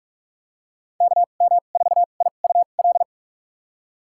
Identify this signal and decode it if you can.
Morse code, heard as a keyed tone, about 46 wpm, spaced as for 33 wpm, 700 Hz. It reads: KM4IUL
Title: USA Call Signs - 6 Characters - 2x Character Spacing - 45wpm